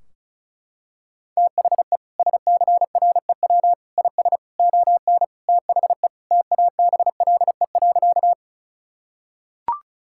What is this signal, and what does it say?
Morse code, heard as a keyed tone, about 36 wpm, 700 Hz. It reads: THE SCREW IS ON THE TABLE.
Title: Sentences from Top 1600 Words 35wpm